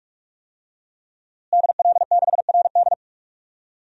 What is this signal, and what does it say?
Morse code, heard as a keyed tone, about 46 wpm, 700 Hz. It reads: DL6RD
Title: DXCC-100 Call Signs - 5 Characters - Repeated 2x - 45wpm